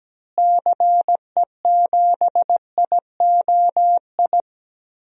Morse code, heard as a keyed tone, about 17 wpm, 700 Hz. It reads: CE7IOI
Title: DXCC-100 Call Signs - 6 Characters - Repeated 3x - 17wpm